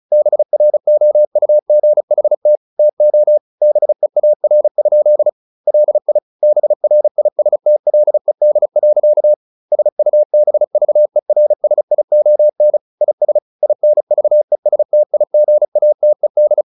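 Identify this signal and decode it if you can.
Morse code, heard as a keyed tone, about 34 wpm, 600 Hz. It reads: BROUGHT TO BEAR? LI BRISTLED. SUBVERSION IS INVESTIGATED
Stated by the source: Morse practice at 35 wpm from CW med Gnister, sending Ken Follett's novel Never in English.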